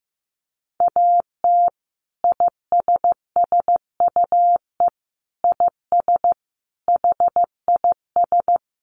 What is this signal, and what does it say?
Morse code, heard as a keyed tone, about 15 wpm, 700 Hz. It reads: AT ISSUE IS HIS
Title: Sentences from Top 900 Words 15wpm